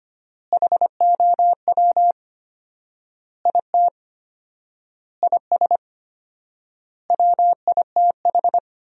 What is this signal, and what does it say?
Morse code, heard as a keyed tone, about 25 wpm, 700 Hz. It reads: HOW IT IS WITH
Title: 4x Word-Spacing - Sentences from Top 100 Words - 25wpm